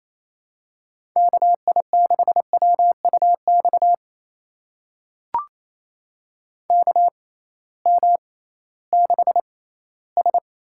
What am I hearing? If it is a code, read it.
Morse code, heard as a keyed tone, about 28 wpm, spaced as for 16 wpm, 700 Hz. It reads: KI6WUX KM6S
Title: USA Call Signs - 6x Character Spacing - 28wpm